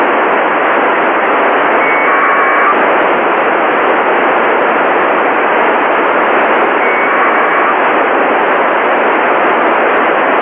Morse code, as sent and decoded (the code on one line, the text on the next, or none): none